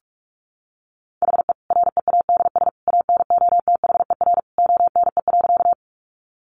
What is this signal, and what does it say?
Morse code, heard as a keyed tone, about 44 wpm, 700 Hz. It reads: HE READS ANOTHER ONE.
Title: Additional Easy Phrases 45wpm